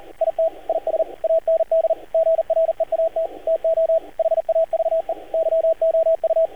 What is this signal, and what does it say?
Morse code, heard as a keyed tone, about 38 wpm, 650 Hz, with radio noise. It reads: IT IS AND GREAT TO HAVE YOU
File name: archived / CRSnet